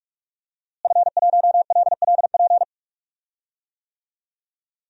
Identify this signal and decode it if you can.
Morse code, heard as a keyed tone, about 43 wpm, 700 Hz. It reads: F1LLP
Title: DXCC-100 Call Signs - 5 Characters - Repeated 2x - 45wpm